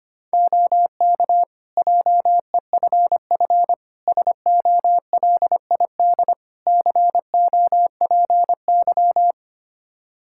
Morse code, heard as a keyed tone, about 25 wpm, 700 Hz. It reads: OK JEFF SOLID COPY
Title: K1USN SST QSO Practice 25wpm